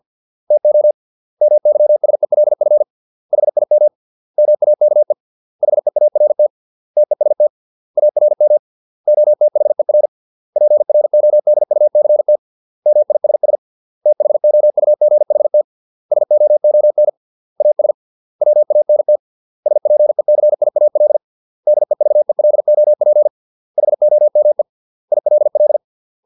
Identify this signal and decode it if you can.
Morse code, heard as a keyed tone, about 50 wpm, 600 Hz. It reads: TO MYSELF HIM MAKE HEART TEST ARM OTHER PRODUCT MISS THOUGHT SOON AS WANT SPECIAL DEVELOP HOME ILL